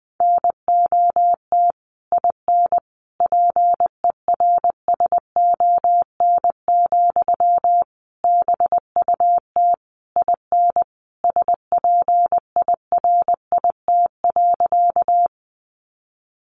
Morse code, heard as a keyed tone, about 20 wpm, 700 Hz. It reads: NOT IN PERSON, BUT IN SPIRIT.
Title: Sentences from Top 1400 Words 20wpm